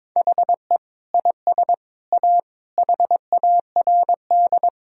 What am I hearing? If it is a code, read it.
Morse code, heard as a keyed tone, about 22 wpm, 700 Hz. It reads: HE IS A HARD
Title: Sentences from Top 1000 Words - Review Part 4 - 22wpm